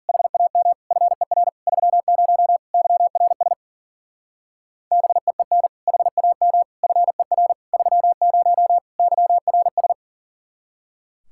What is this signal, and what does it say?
Morse code, heard as a keyed tone, about 49 wpm, 700 Hz. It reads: HAM FER 30 YRS BEEN HAM FER 30 YRS
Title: Speed-Racing Common QSO Phrases - 40wpm